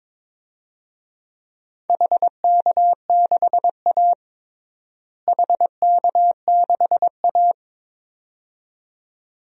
Morse code, heard as a keyed tone, about 22 wpm, 700 Hz. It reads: HK6A HK6A
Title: DXCC-100 Call Signs - 4 Characters - Repeated 2x - 22wpm